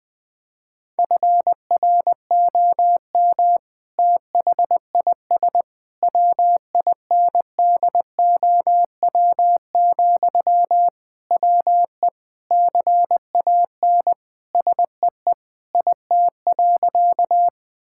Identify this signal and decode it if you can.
Morse code, heard as a keyed tone, about 21 wpm, 700 Hz. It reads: FROM THIS WINDOW, WE CAN SEE IT.
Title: Sentences from Top 1600 Words 20wpm